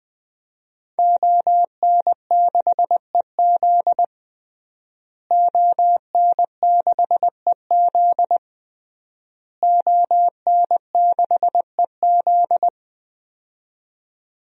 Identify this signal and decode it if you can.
Morse code, heard as a keyed tone, about 20 wpm, 700 Hz. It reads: ON6EZ ON6EZ ON6EZ
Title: DXCC-100 Call Signs - 5 Characters - Repeated 3x - 20wpm